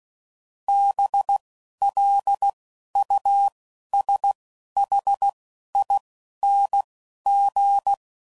Morse code, heard as a keyed tone, about 16 wpm, 800 Hz. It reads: B L U S H I N G
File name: Suffixes / Special INGs / Single INGs